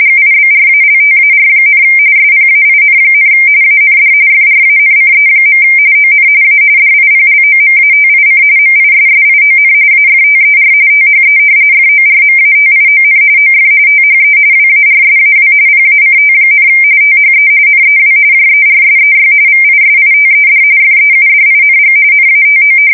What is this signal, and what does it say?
RTTY, about 45.45 baud. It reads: GIBRALTAR, SPAIN, AND FRANCE. ON A DIPLOMATIC MISSION AFTER THE FORMATION OF THE WARSAW PACT, THE NEW JERSEY VISITED FEUDING NATO
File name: fsk170 / battleshipnj